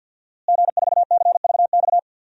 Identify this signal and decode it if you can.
Morse code, heard as a keyed tone, about 52 wpm, 700 Hz. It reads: G4CVX
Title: DXCC-100 Call Signs - 5 Characters - Repeated 2x - 50wpm